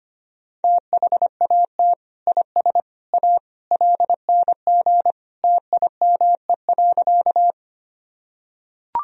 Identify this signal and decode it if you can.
Morse code, heard as a keyed tone, about 25 wpm, 700 Hz. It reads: THAT IS A LNG TIME.
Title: Sentences from Top 100 Words and Abbreviations 25wpm